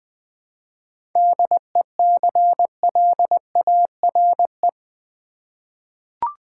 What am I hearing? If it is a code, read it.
Morse code, heard as a keyed tone, about 20 wpm, 700 Hz. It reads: DECLARE
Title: 7 Letter Words - First 3 Letters Given - 20wpm